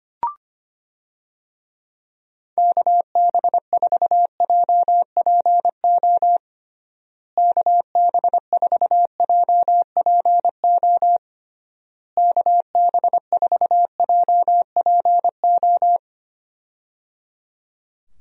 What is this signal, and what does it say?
Morse code, heard as a keyed tone, about 25 wpm, 700 Hz. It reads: KB4JPO KB4JPO KB4JPO
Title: USA Call Signs - 6 Characters - Repeated 3x - 25wpm